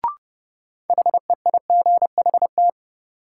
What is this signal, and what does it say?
Morse code, heard as a keyed tone, about 30 wpm, 700 Hz. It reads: HEIGHT
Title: Sentences from Top 1600 Words - New Word Drill Alt - 30wpm